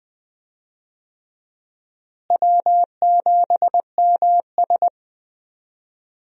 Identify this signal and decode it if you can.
Morse code, heard as a keyed tone, about 20 wpm, 700 Hz. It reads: W7MS
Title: USA Call Signs - 4 Characters - Repeated 3x - 20wpm